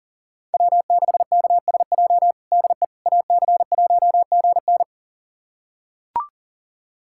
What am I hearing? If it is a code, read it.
Morse code, heard as a keyed tone, about 37 wpm, 700 Hz. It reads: W6KSJ DE AC1GN
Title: USA Callsign DE Callsign - 5 Characters - 40wpm